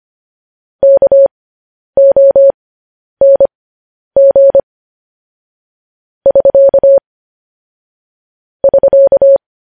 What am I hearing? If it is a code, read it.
Morse code, heard as a keyed tone, about 25 wpm, spaced as for 10 wpm, 550 Hz. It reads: KONG <SK> <SK>
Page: news headlines in morse code at 10 wpm